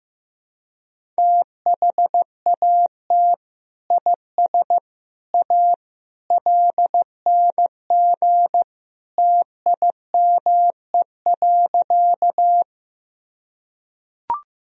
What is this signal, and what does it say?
Morse code, heard as a keyed tone, about 15 wpm, 700 Hz. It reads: THAT IS A LNG TIME.
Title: Sentences from Top 100 Words and Abbreviations 15wpm